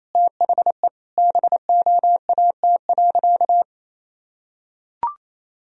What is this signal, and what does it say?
Morse code, heard as a keyed tone, about 28 wpm, 700 Hz. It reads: THE BOAT.
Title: Sentences from Top 1000 Words - Review Part 3 - 28wpm